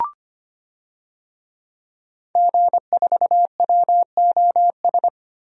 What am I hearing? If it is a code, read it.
Morse code, heard as a keyed tone, about 25 wpm, 700 Hz. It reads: G4WOS
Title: DXCC-100 Call Signs - 5 Characters - 25wpm